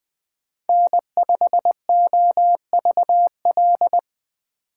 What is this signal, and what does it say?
Morse code, heard as a keyed tone, about 24 wpm, 700 Hz. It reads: N5OVL
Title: Speed Racing - USA Call Signs 20wpm